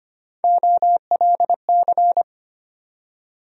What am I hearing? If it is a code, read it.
Morse code, heard as a keyed tone, about 25 wpm, 700 Hz. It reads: OLC
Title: USA Call Sign Postfixes - 3 Letters 25wpm